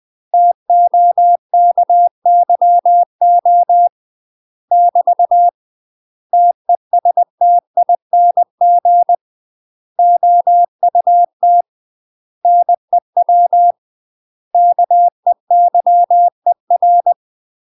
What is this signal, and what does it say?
Morse code, heard as a keyed tone, about 20 wpm, 700 Hz. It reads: TOKYO <BT> TESTING OUT NEW KEYER
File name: UK MASTER QSOs TRAINING / 20WPM / UK MASTER QSOs 0067